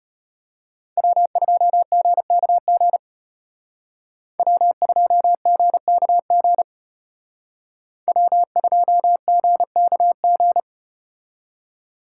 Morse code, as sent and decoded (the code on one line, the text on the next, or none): .-- ..--- --. -.- --. / .-- ..--- --. -.- --. / .-- ..--- --. -.- --.
W2GKG W2GKG W2GKG